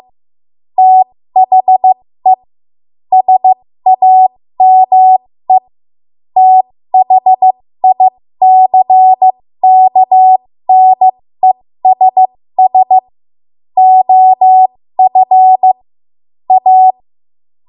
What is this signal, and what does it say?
Morse code, heard as a keyed tone, about 15 wpm, spaced as for 13 wpm, 750 Hz. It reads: THE SAME THICKNESS OF A
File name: Morse / 13WPM